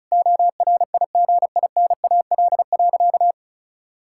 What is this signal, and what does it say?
Morse code, heard as a keyed tone, about 36 wpm, 700 Hz. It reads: ORIGINAL.
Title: Sentences from Top 1000 Words - Review Part 6 - 35wpm